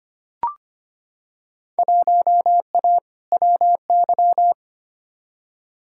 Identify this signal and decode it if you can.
Morse code, heard as a keyed tone, about 25 wpm, 700 Hz. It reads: 1A WY
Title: Field Day Exchanges Encore - 25wpm